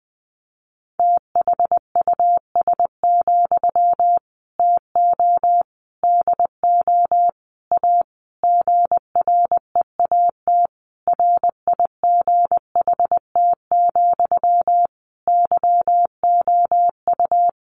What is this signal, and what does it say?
Morse code, heard as a keyed tone, about 20 wpm, 700 Hz. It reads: THUS, TO DO A GREAT RIGHT, YOU
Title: Sentences from Top 900 Words 20wpm